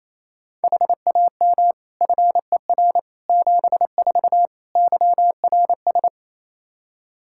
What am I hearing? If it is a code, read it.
Morse code, heard as a keyed tone, about 28 wpm, 700 Hz. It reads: HAM FER 74 YRS
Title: Practice QSO - Exchange 3 - More Info - 28wpm